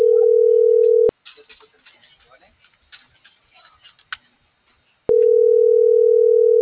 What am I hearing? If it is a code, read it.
Ringback tone (North America), 440 + 480 Hz.
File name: CENTRAL GENERAL / out arturo.vega